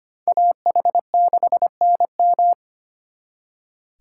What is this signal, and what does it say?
Morse code, heard as a keyed tone, about 32 wpm, 700 Hz. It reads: AH6NM